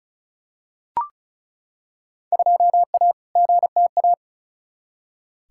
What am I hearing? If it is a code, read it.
Morse code, heard as a keyed tone, about 36 wpm, 700 Hz. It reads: 2A GTA
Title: Field Day Exchanges Encore - 35wpm